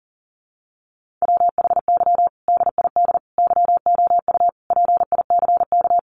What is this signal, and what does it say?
Morse code, heard as a keyed tone, about 37 wpm, 700 Hz. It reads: WHY DID YOU PICK
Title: Sentences from Top 1000 Words 40wpm